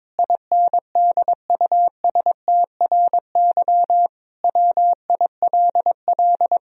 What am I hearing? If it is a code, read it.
Morse code, heard as a keyed tone, about 22 wpm, 700 Hz. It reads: INDUSTRY WILL
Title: Sentences from Top 1000 Words - Review Part 4 - 22wpm